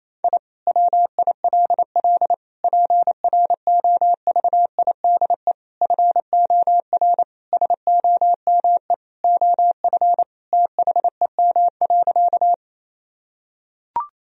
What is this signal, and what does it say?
Morse code, heard as a keyed tone, about 28 wpm, 700 Hz. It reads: I WILL PROVIDE FOR SOME OF THEM.
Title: Sentences from Top 1000 Words - Review Part 3 - 28wpm